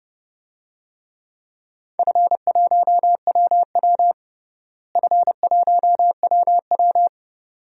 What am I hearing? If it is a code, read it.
Morse code, heard as a keyed tone, about 30 wpm, 700 Hz. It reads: F1WW F1WW
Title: DXCC-100 Call Signs - 4 Characters - Repeated 2x - 30wpm